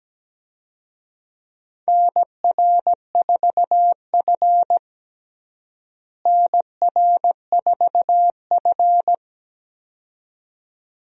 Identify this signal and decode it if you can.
Morse code, heard as a keyed tone, about 17 wpm, 700 Hz. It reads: NR4F NR4F
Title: USA Call Signs - 4 Characters - Repeated 2x - 17wpm